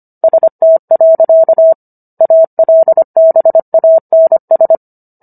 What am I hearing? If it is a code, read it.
Morse code, heard as a keyed tone, about 25 wpm, 650 Hz. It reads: ST. ALBANS